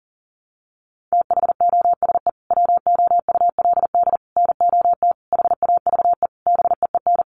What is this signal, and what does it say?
Morse code, heard as a keyed tone, about 40 wpm, 700 Hz. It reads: THOSE WOULD NOT HAVE BEEN
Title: Sentences from Top 1300 Words 40wpm